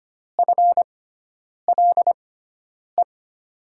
Morse code, heard as a keyed tone, about 25 wpm, spaced as for 9 wpm, 700 Hz. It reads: FLE
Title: USA Call Signs - 6x Character Spacing - 25wpm